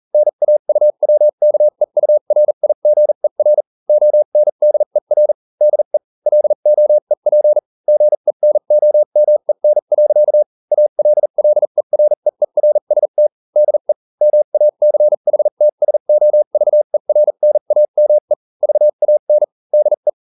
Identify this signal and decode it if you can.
Morse code, heard as a keyed tone, about 40 wpm, spaced as for 35 wpm, 600 Hz. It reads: NAUWKEURIGER ONDER DE LOEP GENOMEN. ALLEREERST DE MACHTSOVERNAME VAN DE